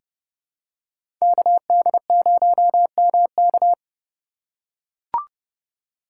Morse code, heard as a keyed tone, about 30 wpm, 700 Hz. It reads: KD0MK
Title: USA Call Signs - 5 Characters - 3x Character Spacing - 30wpm